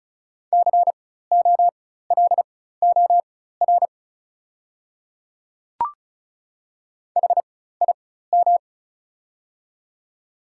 Morse code, heard as a keyed tone, about 36 wpm, spaced as for 16 wpm, 690 Hz. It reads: COLOR HIM
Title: 4x Character Spacing - Top 500 Words - 35wpm